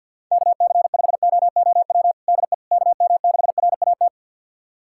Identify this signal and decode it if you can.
Morse code, heard as a keyed tone, about 51 wpm, 700 Hz. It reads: KC5OOW DE KM6RAT
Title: USA Callsign DE Callsign - 6 Characters - 50wpm